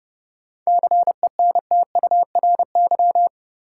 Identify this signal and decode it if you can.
Morse code, heard as a keyed tone, about 30 wpm, 700 Hz. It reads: CENTURY
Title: Sets of 5 Words - New Word Drill - 30wpm